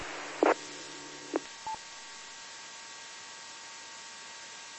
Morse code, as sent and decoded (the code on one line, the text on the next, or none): none